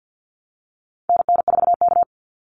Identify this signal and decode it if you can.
Morse code, heard as a keyed tone, about 50 wpm, 700 Hz. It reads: NN4K
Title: Call Signs Repeated 3x - 50wpm